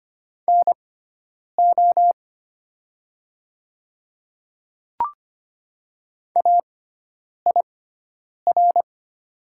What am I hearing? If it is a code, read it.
Morse code, heard as a keyed tone, about 25 wpm, spaced as for 9 wpm, 700 Hz. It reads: NO AIR
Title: 6x Character Spacing - Top 200 Words - 25wpm